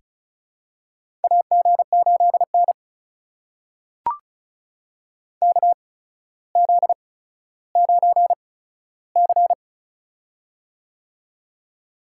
Morse code, heard as a keyed tone, about 36 wpm, spaced as for 15 wpm, 700 Hz. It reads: AG8N KZ9C
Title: USA Call Signs - 4 Characters - 8x Character Spacing - 35wpm